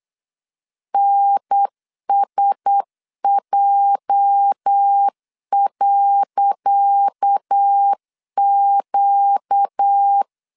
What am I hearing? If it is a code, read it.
Morse code, heard as a keyed tone, about 8 wpm, 800 Hz. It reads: NSJ.Q
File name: LEZIONI / 10 SIGNS / 10C DOT 8wpm 3649